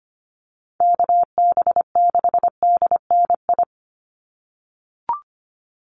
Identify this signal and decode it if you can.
Morse code, heard as a keyed tone, about 25 wpm, 700 Hz. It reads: KB6DNI
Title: USA Call Signs 25wpm